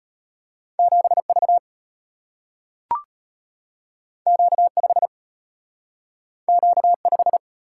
Morse code, heard as a keyed tone, about 34 wpm, 700 Hz. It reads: 7V Q5 Q5